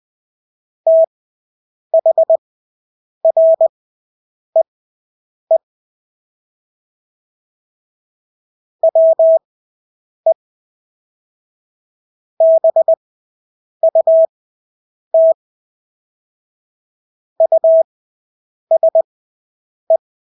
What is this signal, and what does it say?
Morse code, heard as a keyed tone, about 20 wpm, spaced as for 8 wpm, 650 Hz. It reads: THREE WE BUT USE
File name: Engelske ord 20/8 wpm